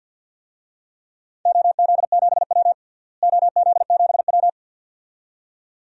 Morse code, heard as a keyed tone, about 48 wpm, 700 Hz. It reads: OZ7W OZ7W
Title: DXCC-100 Call Signs - 4 Characters - Repeated 2x - 50wpm